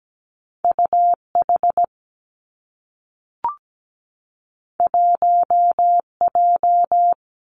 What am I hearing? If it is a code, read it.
Morse code, heard as a keyed tone, about 17 wpm, 700 Hz. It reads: U H 1 J
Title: Two Challenging Letter-Number Combinations 17wpm